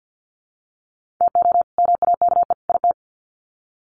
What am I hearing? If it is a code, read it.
Morse code, heard as a keyed tone, about 50 wpm, 700 Hz. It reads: TO MAKE IT
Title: Sets of 3 Words 50wpm